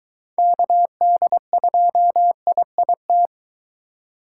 Morse code, heard as a keyed tone, about 22 wpm, 700 Hz. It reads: KD2IIT